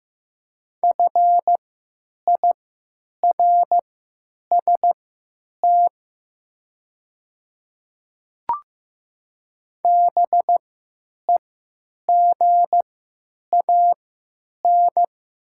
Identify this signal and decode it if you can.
Morse code, heard as a keyed tone, about 15 wpm, spaced as for 9 wpm, 700 Hz. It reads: FIRST BEGAN
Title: 3x Character Spacing - Top 300 Words - 15wpm